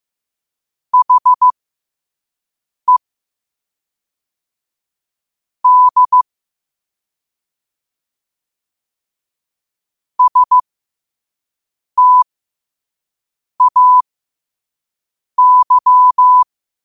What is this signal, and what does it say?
Morse code, heard as a keyed tone, about 15 wpm, spaced as for 5 wpm, 1000 Hz. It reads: HE D STAY
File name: qotdfiles / 5wpm